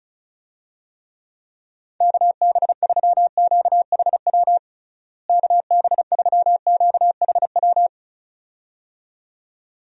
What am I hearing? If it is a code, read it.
Morse code, heard as a keyed tone, about 34 wpm, 700 Hz. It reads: KB3QHW KB3QHW
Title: USA Call Signs - 6 Characters - Repeated 2x - 35wpm